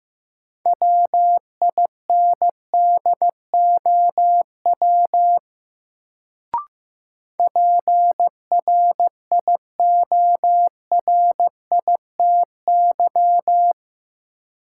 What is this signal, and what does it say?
Morse code, heard as a keyed tone, about 15 wpm, 700 Hz. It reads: WINDOW PRIORITY
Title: Sentences from Top 1600 Words - New Word Drill - 15wpm